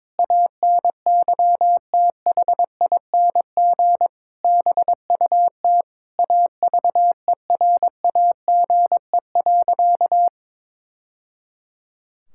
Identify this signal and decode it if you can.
Morse code, heard as a keyed tone, about 22 wpm, 700 Hz. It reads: ANYTHING BUT AVERAGE.
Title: Sentences from Top 1000 Words - Review Part 6 - 22wpm